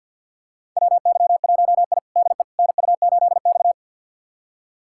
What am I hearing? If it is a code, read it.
Morse code, heard as a keyed tone, about 52 wpm, 700 Hz. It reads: WY1I DE NU8X
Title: USA Callsign DE Callsign - 4 Characters - 50wpm